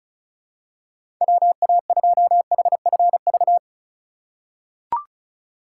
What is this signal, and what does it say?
Morse code, heard as a keyed tone, about 36 wpm, 700 Hz. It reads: WA2HFV
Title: USA Call Signs - 6 Characters - 3x Character Spacing - 35wpm